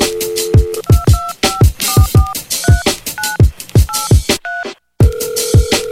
DTMF, heard as telephone touch tones, tones of about 200 ms. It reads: 225443983